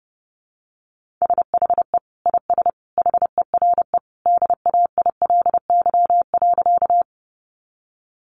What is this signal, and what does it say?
Morse code, heard as a keyed tone, about 30 wpm, 700 Hz. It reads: SHE IS HERE DAILY.
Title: Sentences from Top 1100 Words 30wpm